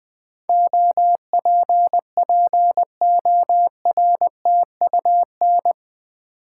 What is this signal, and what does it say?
Morse code, heard as a keyed tone, about 20 wpm, 700 Hz. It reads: OPPORTUN
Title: Build A Word - 4 Letters At A Time - 11-Letter Words - 20wpm